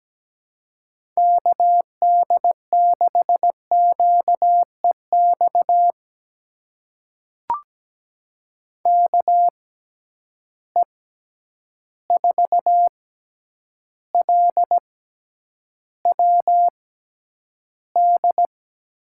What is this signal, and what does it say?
Morse code, heard as a keyed tone, about 17 wpm, spaced as for 9 wpm, 700 Hz. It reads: KD6QEX KE4LWD
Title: USA Call Signs - 6 Characters - 6x Character Spacing - 17wpm